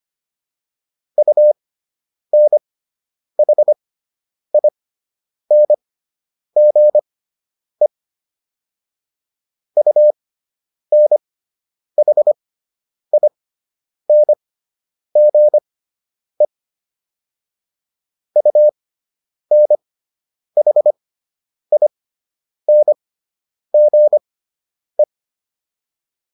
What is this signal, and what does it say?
Morse code, heard as a keyed tone, about 25 wpm, spaced as for 9 wpm, 600 Hz. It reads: UNHINGE UNHINGE UNHINGE